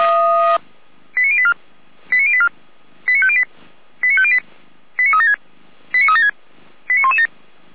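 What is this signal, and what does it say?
Five-tone selective call (ZVEI), heard as quick runs of five tones, tones of about 70 ms. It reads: 89084 89084 89508 89508 89376 89376 89117
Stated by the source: ZVEI five-tone selective call received with an SDR.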